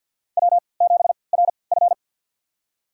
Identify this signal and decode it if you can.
Morse code, heard as a keyed tone, about 49 wpm, spaced as for 28 wpm, 700 Hz. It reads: W7RF
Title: USA Call Signs - 3x Character Spacing - 50wpm